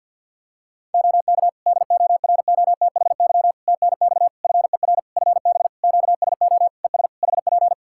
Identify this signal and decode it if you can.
Morse code, heard as a keyed tone, about 53 wpm, 700 Hz. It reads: OK DOROTHY TNX FER FB QSO ES HP
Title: Practice QSO - Exchange 4 - Ending - 50wpm